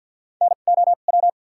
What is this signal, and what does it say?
Morse code, heard as a keyed tone, about 52 wpm, spaced as for 36 wpm, 690 Hz. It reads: NOW
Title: 2x Character Spacing - Top 100 Words - 50wpm